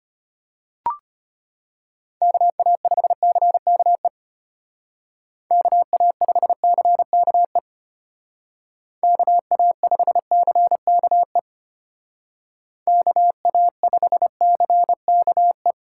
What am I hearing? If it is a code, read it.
Morse code, heard as a keyed tone, about 30 wpm, 700 Hz. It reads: KA5CKE KA5CKE KA5CKE KA5CKE
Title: Speed Racing - USA Call Signs 25wpm